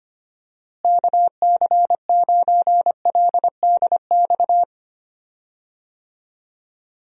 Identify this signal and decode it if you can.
Morse code, heard as a keyed tone, about 25 wpm, 700 Hz. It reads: KC9LDX
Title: USA Call Signs - 6 Characters - Repeated 3x - 25wpm